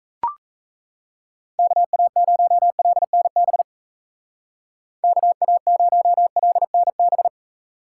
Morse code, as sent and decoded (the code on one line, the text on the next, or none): -.- .- ----- .-.. -. -... / -.- .- ----- .-.. -. -...
KA0LNB KA0LNB